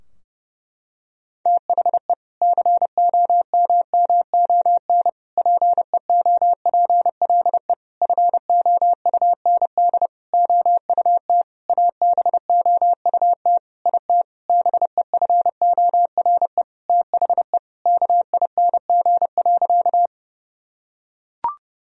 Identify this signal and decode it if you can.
Morse code, heard as a keyed tone, about 30 wpm, 700 Hz. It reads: THE COMMON PEOPLE FOUND OUT ABOUT IT BEFORE THE KING.
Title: Sentences from Top 1000 Words - Review Part 3 - 30wpm